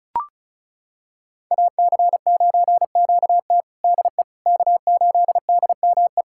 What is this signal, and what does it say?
Morse code, heard as a keyed tone, about 36 wpm, 700 Hz. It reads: AC9QT DE K8DME
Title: USA Callsign DE Callsign - 5 Characters - 35wpm